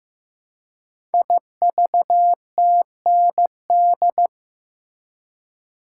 Morse code, heard as a keyed tone, about 15 wpm, 700 Hz. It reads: IVTND
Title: Code Groups - 5 Letters - 15wpm